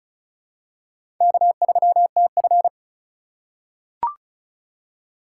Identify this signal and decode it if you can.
Morse code, heard as a keyed tone, about 36 wpm, 700 Hz. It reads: K3TF
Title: K1USN SST QSO Practice - 35wpm